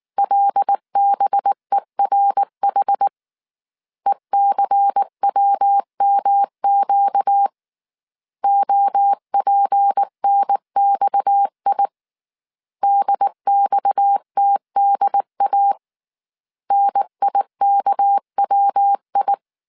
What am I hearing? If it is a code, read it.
Morse code, heard as a keyed tone, about 19 wpm, 800 Hz. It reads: LBERH ECWMQ OPNXI DXTDA NIKWI
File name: CW / DET / 1 6 1500C S C 18wpm 2356